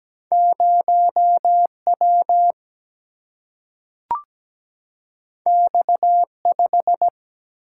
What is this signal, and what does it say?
Morse code, heard as a keyed tone, about 17 wpm, 700 Hz. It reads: 0W X5
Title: Single-Letter Single-Number 17wpm